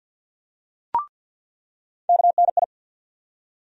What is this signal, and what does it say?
Morse code, heard as a keyed tone, about 47 wpm, 700 Hz. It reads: KNI
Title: USA Call Sign Postfixes - 3 Letters 50wpm